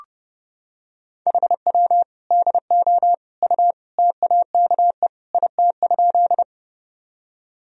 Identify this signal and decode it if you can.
Morse code, heard as a keyed tone, about 30 wpm, 700 Hz. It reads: HW DO U TAKE IT?
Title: Sentences from Top 200 Words and Abbreviations 30wpm